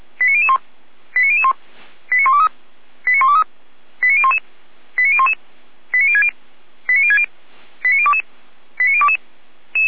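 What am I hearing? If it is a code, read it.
Five-tone selective call (ZVEI), heard as quick runs of five tones, tones of about 70 ms. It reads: 89001 89001 89123 89123 89011 89011 89060 89060 89022 89022
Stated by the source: ZVEI five-tone call received on SDR.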